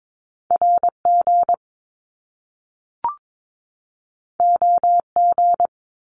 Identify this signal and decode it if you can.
Morse code, heard as a keyed tone, about 22 wpm, 700 Hz. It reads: RG OG
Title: Top Two Letter Combinations in Words 22wpm